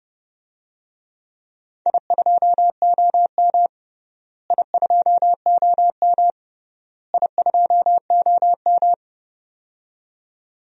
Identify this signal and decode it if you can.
Morse code, heard as a keyed tone, about 30 wpm, 700 Hz. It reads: I2OM I2OM I2OM
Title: DXCC-100 Call Signs - 4 Characters - Repeated 3x - 30wpm